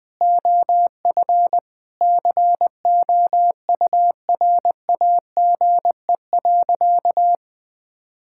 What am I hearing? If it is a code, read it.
Morse code, heard as a keyed tone, about 20 wpm, 700 Hz. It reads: OF COURAGE.
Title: Sentences with 4-Word Phrases - Spoken Beginning - 20wpm